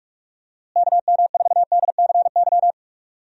Morse code, heard as a keyed tone, about 48 wpm, 700 Hz. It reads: KM4DCY
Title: USA Call Signs - 6 Characters - Repeated 2x - 45wpm